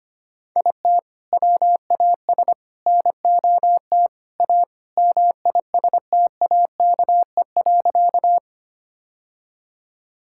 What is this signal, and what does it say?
Morse code, heard as a keyed tone, about 25 wpm, 700 Hz. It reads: IT WAS NOT A MISTAKE.
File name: practice / 25wpm / Sentences with 4-Word Phrases - 25wpm